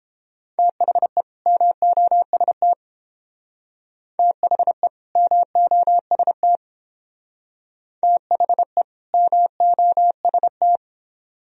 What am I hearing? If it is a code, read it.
Morse code, heard as a keyed tone, about 28 wpm, 700 Hz. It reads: THE MOST THE MOST THE MOST